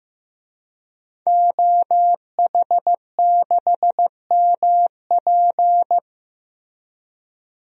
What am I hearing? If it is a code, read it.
Morse code, heard as a keyed tone, about 16 wpm, 700 Hz. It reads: OH6MP
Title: DXCC-100 Call Signs - 5 Characters - Repeated 3x - 15wpm